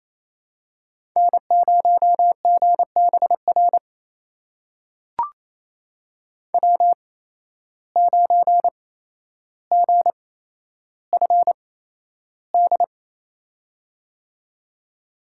Morse code, heard as a keyed tone, about 28 wpm, spaced as for 12 wpm, 700 Hz. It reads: N0GBR W9GFD
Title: USA Call Signs - 8x Character Spacing - 28wpm